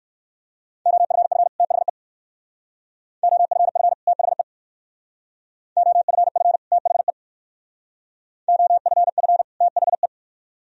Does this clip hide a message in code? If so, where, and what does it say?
Yes: Morse code from 0.9 to 10.1 s, heard as a keyed tone, about 69 wpm, 700 Hz. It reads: OFF THE OFF THE OFF THE OFF THE